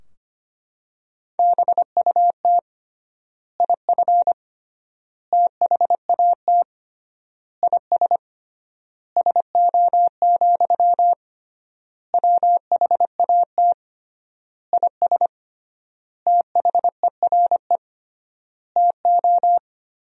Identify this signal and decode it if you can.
Morse code, heard as a keyed tone, about 25 wpm, 700 Hz. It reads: BUT IF THAT IS SO, WHAT IS THERE TO
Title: 3x Word-Spacing - Sentences from Top 200 Words - 25wpm